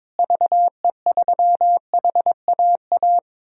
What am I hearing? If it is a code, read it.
Morse code, heard as a keyed tone, about 22 wpm, 700 Hz. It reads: VE3HAA
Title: DXCC-100 Call Signs - 6 Characters - Repeated 2x - 22wpm